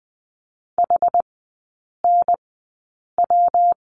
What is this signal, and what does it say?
Morse code, heard as a keyed tone, about 20 wpm, spaced as for 8 wpm, 690 Hz. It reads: HNW
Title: 172 Lesson 28e - Number 9 - Three Characters 20wpm